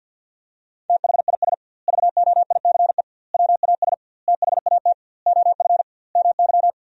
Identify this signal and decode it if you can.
Morse code, heard as a keyed tone, about 46 wpm, 700 Hz. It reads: THIS VOICE WAS THAT OF MY
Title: 2x Word-Spacing - Sentences from Top 500 Words - 50wpm